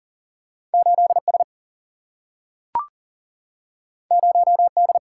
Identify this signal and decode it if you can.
Morse code, heard as a keyed tone, about 40 wpm, 700 Hz. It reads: 8 S 0 D
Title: Single-Letter Single-Number 40wpm